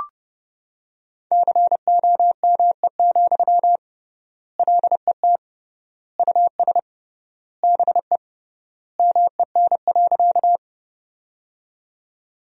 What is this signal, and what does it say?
Morse code, heard as a keyed tone, about 30 wpm, 700 Hz. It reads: COME, LET US BE MEN.
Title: 3x Word-Spacing - Sentences from Top 300 Words - 30wpm